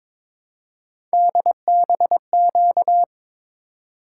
Morse code, heard as a keyed tone, about 22 wpm, 700 Hz. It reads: DBQ
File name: practice / 22wpm / Three Letter-Number Combinations - 22wpm